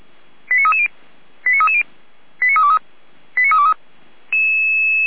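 Five-tone selective call (ZVEI), heard as quick runs of five tones, tones of about 70 ms. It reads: 89330 89330 89323 89323
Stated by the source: ZVEI five-tone selective call received on SDR.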